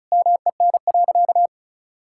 Morse code, heard as a keyed tone, about 36 wpm, 700 Hz. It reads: MEN.
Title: Sentences from Top 1000 Words - Review Part 6 - 35wpm